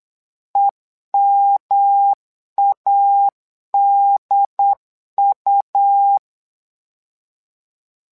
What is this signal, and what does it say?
Morse code, heard as a keyed tone, about 8 wpm, 800 Hz. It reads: EMADU
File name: LEZIONI / 3 MWRO / 3 MWRO 8wpm 2606